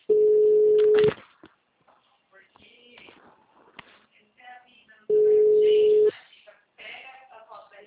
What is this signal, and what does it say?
Ringback tone (Europe, 425 Hz), with speech.